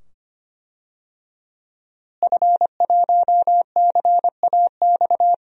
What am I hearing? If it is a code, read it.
Morse code, heard as a keyed tone, about 25 wpm, 700 Hz. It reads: F1CAX
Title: DXCC-100 Call Signs - 5 Characters - Repeated 2x - 25wpm